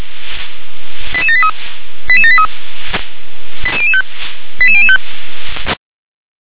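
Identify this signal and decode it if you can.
Five-tone selective call (ZVEI), heard as quick runs of five tones, tones of about 70 ms. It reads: 88783 88783 88005 88005
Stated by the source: ZVEI five-tone selective call received by SDR.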